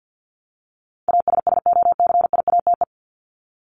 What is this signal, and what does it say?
Morse code, heard as a keyed tone, about 50 wpm, 700 Hz. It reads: ASSOCIATE
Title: ED Suffix - Top 100 Words with Base Words 50wpm